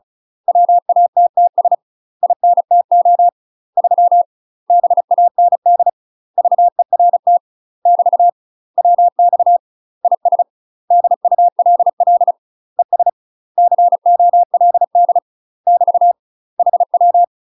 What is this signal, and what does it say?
Morse code, heard as a keyed tone, about 36 wpm, 700 Hz. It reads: WATTS INTO 3 BAND VERT <BT> WX IS DULL ES COLD <BT> HW
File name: UK MASTER QSOs TRAINING / 35WPM / UK MASTER QSOs 0002